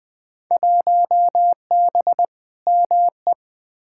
Morse code, heard as a keyed tone, about 21 wpm, 700 Hz. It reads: 1B ME
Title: Field Day Exchanges - 20wpm